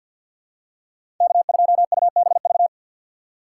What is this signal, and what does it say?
Morse code, heard as a keyed tone, about 52 wpm, 700 Hz. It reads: K2UBV
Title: USA Call Signs - 5 Characters - 3x Character Spacing - 50wpm